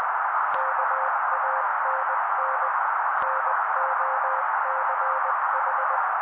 Morse code, heard as a keyed tone, about 20 wpm, 530 Hz. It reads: KANN NOCH
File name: daily / 18wpm noise